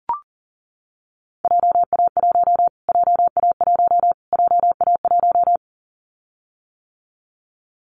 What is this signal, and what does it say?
Morse code, heard as a keyed tone, about 40 wpm, 700 Hz. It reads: JA1 JA1 JA1
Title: Callsign Prefixes for Top 25 DXCC - Repeated 3x - 40wpm